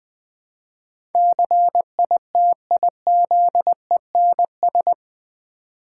Morse code, heard as a keyed tone, about 20 wpm, 700 Hz. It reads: CITIZENS
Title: Build A Word - 4 Letters At A Time - 11-Letter Words - 20wpm